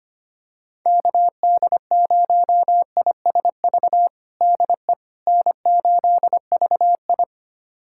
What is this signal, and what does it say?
Morse code, heard as a keyed tone, about 25 wpm, 700 Hz. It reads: KD0ISV DE N8VI
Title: USA Callsign DE Callsign 25wpm